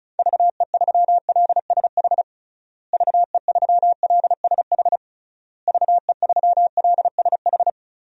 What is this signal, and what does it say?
Morse code, heard as a keyed tone, about 34 wpm, 700 Hz. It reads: VE3LSH VE3LSH VE3LSH
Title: DXCC-100 Call Signs - 6 Characters - Repeated 3x - 35wpm